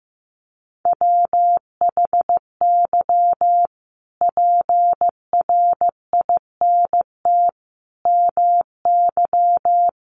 Morse code, heard as a keyed tone, about 15 wpm, 700 Hz. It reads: WHY PRINT MY
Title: Sentences from Top 1400 Words 15wpm